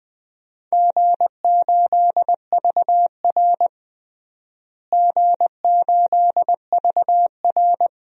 Morse code, heard as a keyed tone, about 20 wpm, 700 Hz. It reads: G8VR G8VR
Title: DXCC-100 Call Signs - 4 Characters - Repeated 2x - 20wpm